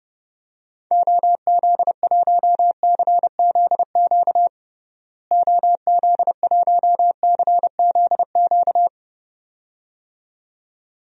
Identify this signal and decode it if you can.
Morse code, heard as a keyed tone, about 30 wpm, 700 Hz. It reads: OZ1CZQ OZ1CZQ
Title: DXCC-100 Call Signs - 6 Characters - Repeated 2x - 30wpm